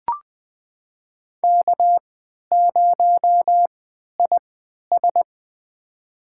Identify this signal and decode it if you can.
Morse code, heard as a keyed tone, about 20 wpm, spaced as for 11 wpm, 700 Hz. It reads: K0IS
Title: USA Call Signs - 3x Character Spacing - 20wpm